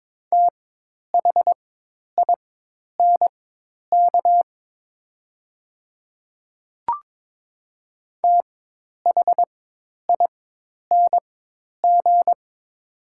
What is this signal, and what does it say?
Morse code, heard as a keyed tone, about 22 wpm, spaced as for 10 wpm, 700 Hz. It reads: THINK THING
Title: 4x Character Spacing - Top 300 Words - 22wpm